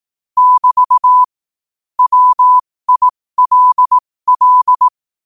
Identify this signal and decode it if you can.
Morse code, heard as a keyed tone, about 18 wpm, spaced as for 15 wpm, 1000 Hz. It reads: <BT> WILL
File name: qotdfiles / 15wpm